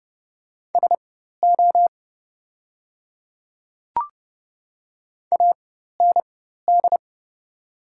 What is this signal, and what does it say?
Morse code, heard as a keyed tone, about 30 wpm, spaced as for 14 wpm, 700 Hz. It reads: SO AND
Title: 4x Character Spacing - Top 100 Words - 30wpm